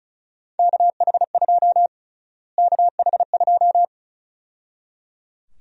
Morse code, heard as a keyed tone, about 34 wpm, 700 Hz. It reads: KH2 KH2
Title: USA Call Sign Prefixes - Repeated 2x - 35wpm